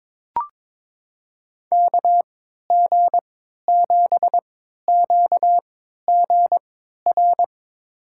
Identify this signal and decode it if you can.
Morse code, heard as a keyed tone, about 22 wpm, spaced as for 12 wpm, 700 Hz. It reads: KG7QGR
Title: USA Call Signs - 6 Characters - 3x Character Spacing - 22wpm